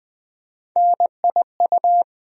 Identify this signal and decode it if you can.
Morse code, heard as a keyed tone, about 21 wpm, 700 Hz. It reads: NIU
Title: USA Call Sign Postfixes - 3 Letters 20wpm